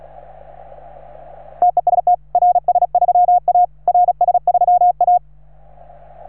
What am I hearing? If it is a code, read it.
Morse code, heard as a keyed tone, about 36 wpm, 710 Hz. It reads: TEST RS3A RS3A
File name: RS3A-20